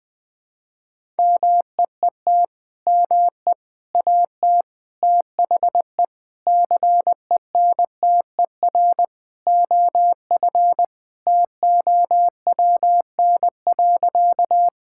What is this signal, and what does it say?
Morse code, heard as a keyed tone, about 21 wpm, 700 Hz. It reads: MEET ME AT THE CENTER OF TOWN.
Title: Sentences with 3-Word Phrases - Spoken Beginning - 20wpm